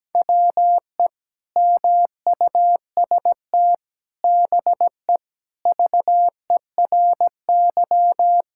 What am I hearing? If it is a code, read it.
Morse code, heard as a keyed tone, about 17 wpm, 700 Hz. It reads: WE MUST BE VERY
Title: Sentences from Top 1600 Words 17wpm